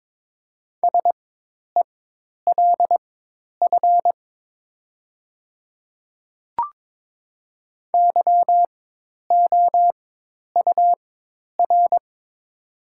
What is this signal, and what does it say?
Morse code, heard as a keyed tone, about 22 wpm, spaced as for 10 wpm, 700 Hz. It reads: SELF YOUR
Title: 4x Character Spacing - Top 200 Words - 22wpm